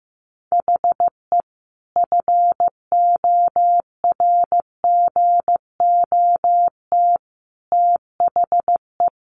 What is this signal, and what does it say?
Morse code, heard as a keyed tone, about 15 wpm, 700 Hz. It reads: HE FORGOT THE
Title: Sentences from Top 1400 Words 15wpm